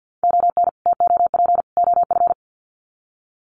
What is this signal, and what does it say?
Morse code, heard as a keyed tone, about 50 wpm, 700 Hz. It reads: ON TOP OF
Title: Sets of 3 Words 50wpm